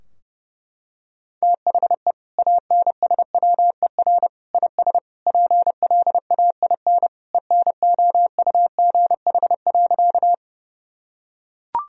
Morse code, heard as a keyed tone, about 30 wpm, 700 Hz. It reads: THE ANSWER IS PLAIN ENOUGH.
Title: Sentences from Top 1000 Words - Review Part 2 - 30wpm